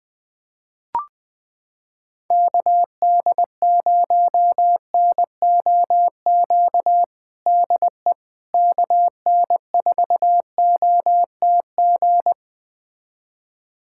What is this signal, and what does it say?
Morse code, heard as a keyed tone, about 20 wpm, 700 Hz. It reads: KD0NOQ DE KN4OTG
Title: USA Callsign DE Callsign - 6 Characters - 20wpm